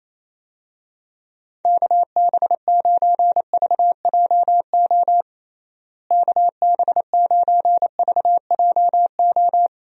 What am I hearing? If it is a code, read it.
Morse code, heard as a keyed tone, about 27 wpm, 700 Hz. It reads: KB9VJO KB9VJO
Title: USA Call Signs - 6 Characters - Repeated 3x - 28wpm